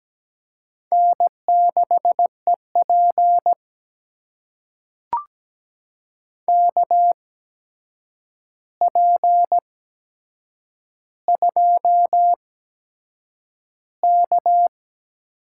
Morse code, heard as a keyed tone, about 17 wpm, spaced as for 7 wpm, 700 Hz. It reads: N6EP KP2K
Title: USA Call Signs - 4 Characters - 8x Character Spacing - 17wpm